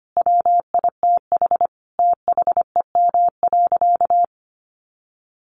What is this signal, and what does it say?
Morse code, heard as a keyed tone, about 25 wpm, 700 Hz. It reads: WITH THEM.
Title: Sentences from Top 800 Words 25wpm